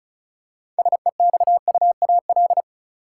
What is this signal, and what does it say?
Morse code, heard as a keyed tone, about 36 wpm, 700 Hz. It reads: SEXUAL
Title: Six Letter Words - 35wpm